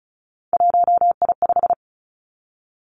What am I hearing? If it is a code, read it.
Morse code, heard as a keyed tone, about 36 wpm, 700 Hz. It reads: 1I5
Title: Three Challenging Letter-Number Combinations 35wpm